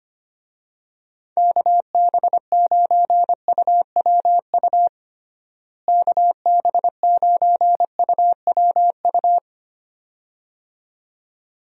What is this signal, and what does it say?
Morse code, heard as a keyed tone, about 25 wpm, 700 Hz. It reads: KB9UWU KB9UWU
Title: USA Call Signs - 6 Characters - Repeated 2x - 25wpm